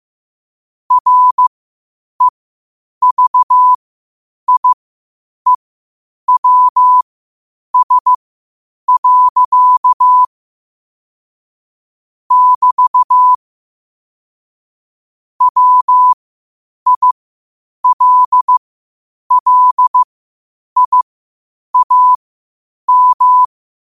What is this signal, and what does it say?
Morse code, heard as a keyed tone, about 15 wpm, spaced as for 8 wpm, 1000 Hz. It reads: REVIEWS. <BT> WILLIAM
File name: qotdfiles / 8wpm